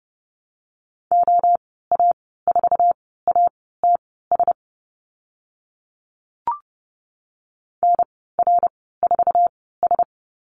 Morse code, heard as a keyed tone, about 30 wpm, spaced as for 17 wpm, 700 Hz. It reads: OA4ATS NR4S
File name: Lesson 14g - Number 2 - Call Signs 30wpm